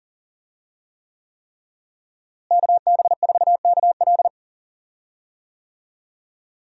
Morse code, heard as a keyed tone, about 40 wpm, 700 Hz. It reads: KB4KL
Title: USA Call Signs - 5 Characters - Repeated 3x - 40wpm